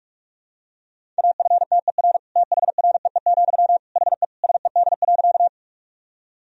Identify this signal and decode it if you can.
Morse code, heard as a keyed tone, about 41 wpm, 700 Hz. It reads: AFTER THREE, HE SED.
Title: Sentences from Top 200 Words and Abbreviations 45wpm